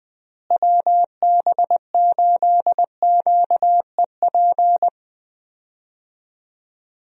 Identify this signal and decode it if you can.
Morse code, heard as a keyed tone, about 20 wpm, 700 Hz. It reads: WB8QEP
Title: USA Call Signs - 6 Characters - Repeated 2x - 20wpm